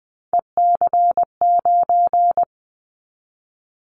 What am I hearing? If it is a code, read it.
Morse code, heard as a keyed tone, about 20 wpm, 700 Hz. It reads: EC9
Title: Callsign Prefixes for Top 100 DXCC - 20wpm